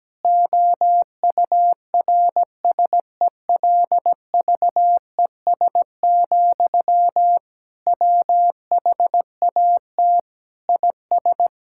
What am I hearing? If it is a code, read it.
Morse code, heard as a keyed tone, about 17 wpm, 700 Hz. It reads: OURSELVES, WHAT IS
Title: Sentences from Top 1600 Words 17wpm